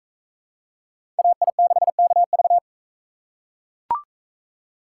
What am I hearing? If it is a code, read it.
Morse code, heard as a keyed tone, about 43 wpm, 700 Hz. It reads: AI6KV